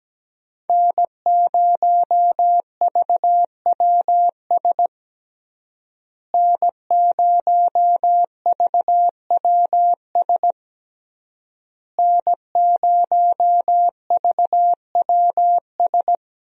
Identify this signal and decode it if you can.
Morse code, heard as a keyed tone, about 17 wpm, 700 Hz. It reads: N0VWS N0VWS N0VWS
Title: USA Call Signs - 5 Characters - Repeated 3x - 17wpm